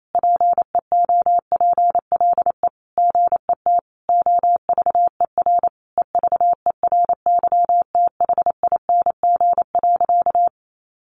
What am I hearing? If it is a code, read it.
Morse code, heard as a keyed tone, about 28 wpm, 700 Hz. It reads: PEOPLE GET OVER EVERYTHING.
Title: Sentences from Top 900 Words 28wpm